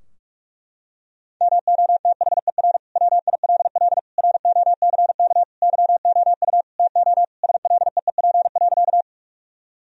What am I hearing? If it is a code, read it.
Morse code, heard as a keyed tone, about 46 wpm, 700 Hz. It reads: MOTHER WILL ROCK YOU TO SLEEP.
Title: Sentences from Top 1000 Words - Review Part 2 - 45wpm